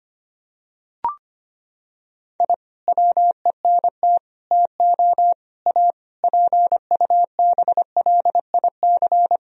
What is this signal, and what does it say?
Morse code, heard as a keyed tone, about 25 wpm, 700 Hz. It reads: I WENT TO A PUBLIC
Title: Sentences from Top 1000 Words - Review Part 3 - 25wpm